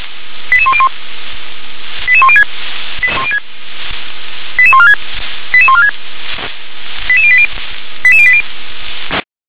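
Five-tone selective call (ZVEI), heard as quick runs of five tones, tones of about 70 ms. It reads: 88101 88106 88146 88146 88080 88080
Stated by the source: ZVEI five-tone selective call received on an SDR.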